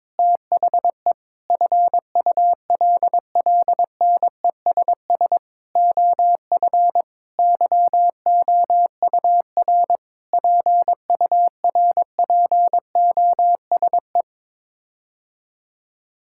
Morse code, heard as a keyed tone, about 21 wpm, 700 Hz. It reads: THE FULLNESS OF YOUR PURPOSE
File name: Sets of 9 Words - 22wpm